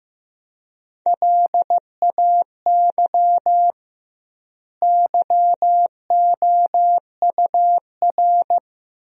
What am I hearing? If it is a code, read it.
Morse code, heard as a keyed tone, about 15 wpm, 700 Hz. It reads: LAY YOUR
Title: 2x Word-Spacing - Sentences from Top 400 Words - 15wpm